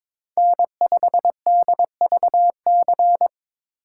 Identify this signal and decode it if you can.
Morse code, heard as a keyed tone, about 22 wpm, 700 Hz. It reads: N5DVC
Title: USA Call Signs - 2x Character Spacing - 22wpm